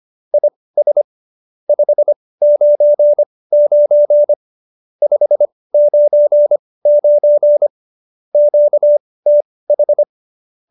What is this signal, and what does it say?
Morse code, heard as a keyed tone, about 25 wpm, spaced as for 18 wpm, 590 Hz. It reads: IS 599 599 QTH
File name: qso216b 18